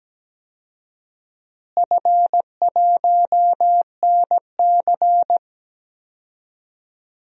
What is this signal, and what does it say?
Morse code, heard as a keyed tone, about 17 wpm, 700 Hz. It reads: F1NC
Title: DXCC-100 Call Signs - 4 Characters - Repeated 2x - 17wpm